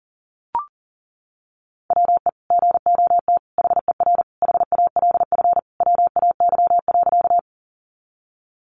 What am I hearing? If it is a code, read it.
Morse code, heard as a keyed tone, about 40 wpm, 700 Hz. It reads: WE GOT HER HALF WAY.
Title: Sentences from Top 400 Words 40wpm